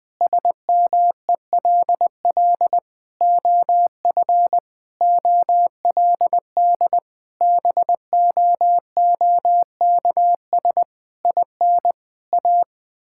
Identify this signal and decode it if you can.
Morse code, heard as a keyed tone, about 20 wpm, 700 Hz. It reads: SMELL OF OLD BOOKS IN A
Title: Difficult Sentences with Hints - 20wpm